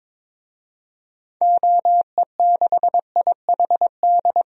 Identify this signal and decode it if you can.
Morse code, heard as a keyed tone, about 22 wpm, 700 Hz. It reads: OE6IHD
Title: DXCC-100 Call Signs - 6 Characters - Repeated 2x - 22wpm